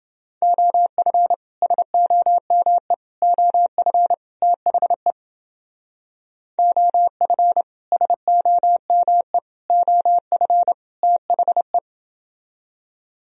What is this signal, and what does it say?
Morse code, heard as a keyed tone, about 29 wpm, 700 Hz. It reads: OF SOME OF THE OF SOME OF THE